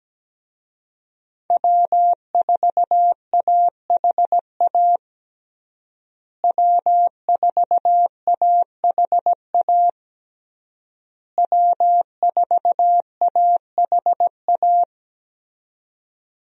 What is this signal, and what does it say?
Morse code, heard as a keyed tone, about 17 wpm, 700 Hz. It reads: W4AHA W4AHA W4AHA
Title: USA Call Signs - 5 Characters - Repeated 3x - 17wpm